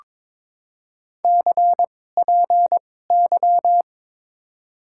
Morse code, heard as a keyed tone, about 22 wpm, spaced as for 16 wpm, 700 Hz. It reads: CPY
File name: Top 90 QSO Elements - 2x Character Spacing - 22wpm